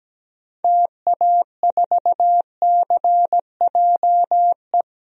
Morse code, heard as a keyed tone, about 17 wpm, 700 Hz. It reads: TA4CJE
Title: DXCC-100 Call Signs - 6 Characters - 17wpm